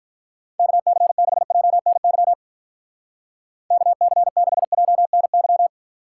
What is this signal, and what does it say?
Morse code, heard as a keyed tone, about 43 wpm, 700 Hz. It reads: KC6JNY KC6JNY
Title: Speed Racing - USA 2x3 Call Signs 35wpm